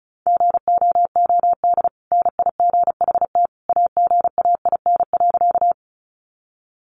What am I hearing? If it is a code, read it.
Morse code, heard as a keyed tone, about 34 wpm, 700 Hz. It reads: GOOD NIGHT AGAIN.
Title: Sentences from Top 300 Words 35wpm